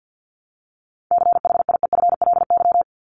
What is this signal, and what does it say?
Morse code, heard as a keyed tone, about 54 wpm, 700 Hz. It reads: CHIEFLY
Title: LY Suffix - Top 100 Words with Base Words 50wpm